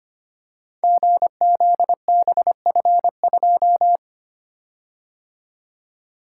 Morse code, heard as a keyed tone, about 25 wpm, 700 Hz. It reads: GZBF2
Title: Code Groups - 5 Letter:Numbers - 25wpm